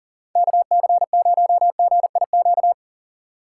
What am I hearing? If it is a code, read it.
Morse code, heard as a keyed tone, about 38 wpm, 700 Hz. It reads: KC0GIQ
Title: USA Call Signs - 6 Characters - 6x Character Spacing - 40wpm